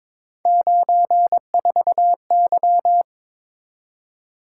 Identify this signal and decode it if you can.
Morse code, heard as a keyed tone, about 22 wpm, 700 Hz. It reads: 94Y
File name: practice / 22wpm / Three Letter-Number Combinations - 22wpm